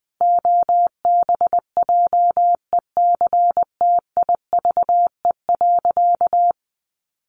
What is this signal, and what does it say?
Morse code, heard as a keyed tone, about 20 wpm, 700 Hz. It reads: OBJECTIVE.
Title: Sentences from Top 1200 Words 20wpm